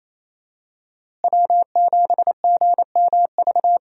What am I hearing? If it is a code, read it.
Morse code, heard as a keyed tone, about 28 wpm, 700 Hz. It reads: W7GMV
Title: USA Call Signs - 5 Characters - Repeated 2x - 28wpm